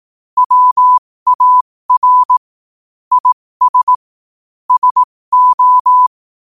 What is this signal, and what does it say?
Morse code, heard as a keyed tone, about 18 wpm, spaced as for 15 wpm, 1000 Hz. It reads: WAR IS SO
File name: qotdfiles / 15wpm